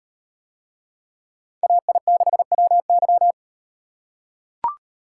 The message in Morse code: AI6WY

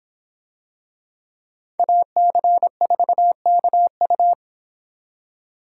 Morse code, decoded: AC4KU

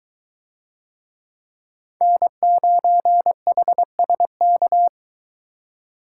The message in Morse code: N9HSK

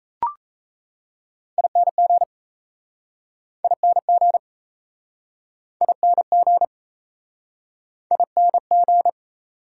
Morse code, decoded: ING ING ING ING

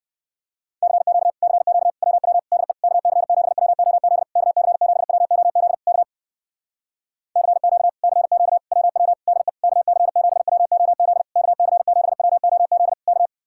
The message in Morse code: CQ CQ WW DE KC7WOZ KC7WOZ K CQ CQ WW DE KC7WOZ KC7WOZ K

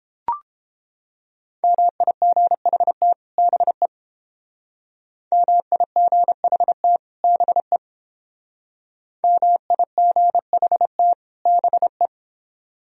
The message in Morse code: MIGHT BE MIGHT BE MIGHT BE

